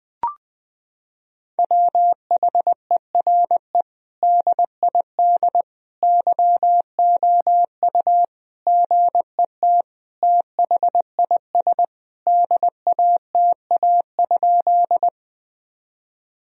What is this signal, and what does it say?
Morse code, heard as a keyed tone, about 20 wpm, 700 Hz. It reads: WHERE DID YOU GET THIS DATA?
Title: Sentences from Top 1000 Words - Review Part 3 - 20wpm